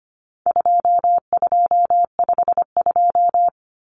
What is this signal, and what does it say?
Morse code, heard as a keyed tone, about 25 wpm, 700 Hz. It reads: 2252
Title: Common Cut Numbers - 4-Digit 25wpm